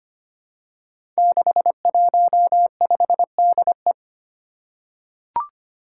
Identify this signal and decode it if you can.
Morse code, heard as a keyed tone, about 25 wpm, 700 Hz. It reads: 615DE
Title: Five Challenging Letter-Number Combinations 25wpm